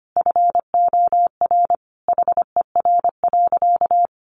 Morse code, heard as a keyed tone, about 25 wpm, 700 Hz. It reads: FOR HER.
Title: Sentences from Top 600 Words 25wpm